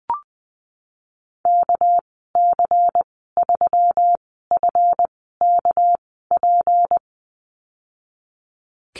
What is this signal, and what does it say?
Morse code, heard as a keyed tone, about 20 wpm, spaced as for 14 wpm, 700 Hz. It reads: KC3FKP
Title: 245 Lesson 39g - Letter z - Call Signs 20wpm